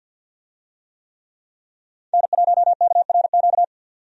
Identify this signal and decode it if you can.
Morse code, heard as a keyed tone, about 53 wpm, 700 Hz. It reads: N1KRQ
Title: USA Call Signs - 5 Characters - 50wpm